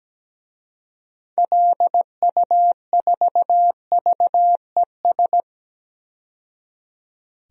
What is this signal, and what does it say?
Morse code, heard as a keyed tone, about 17 wpm, 700 Hz. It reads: LU4VES
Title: DXCC-100 Call Signs - 6 Characters - 17wpm